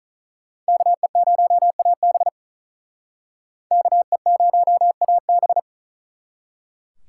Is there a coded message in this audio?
Yes: Morse code from 0.7 to 5.6 s, heard as a keyed tone, about 46 wpm, 700 Hz. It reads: KE0AB KE0AB